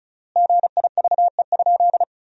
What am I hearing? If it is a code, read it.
Morse code, heard as a keyed tone, about 36 wpm, 700 Hz. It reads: GIVE?
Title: 3x Word-Spacing - Sentences from Top 200 Words - 35wpm